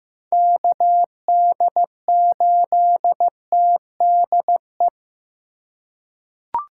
Morse code, heard as a keyed tone, about 16 wpm, 700 Hz. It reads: KD8TDE
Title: USA Call Signs - 2x Character Spacing - 15wpm